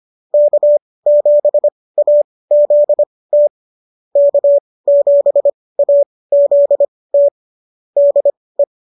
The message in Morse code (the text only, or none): K7AZT K7AZT DE